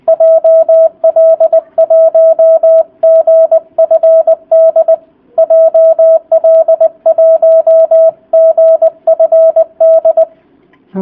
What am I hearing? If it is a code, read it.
Morse code, heard as a keyed tone, about 20 wpm, 650 Hz. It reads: JL1GFD JL1GFD